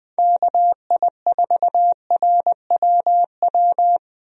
Morse code, heard as a keyed tone, about 20 wpm, 700 Hz. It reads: KI4RWW
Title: USA Call Signs - 6 Characters - Repeated 2x - 20wpm